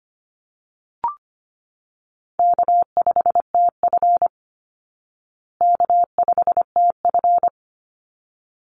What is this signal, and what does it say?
Morse code, heard as a keyed tone, about 25 wpm, 700 Hz. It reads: K5TF K5TF
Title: Easy Call Signs Repeated 2x - 25wpm